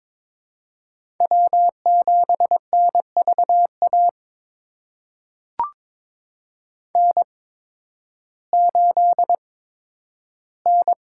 Morse code, heard as a keyed tone, about 22 wpm, spaced as for 12 wpm, 700 Hz. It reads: W7NVA N8N
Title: USA Call Signs - 5 Characters - 8x Character Spacing - 22wpm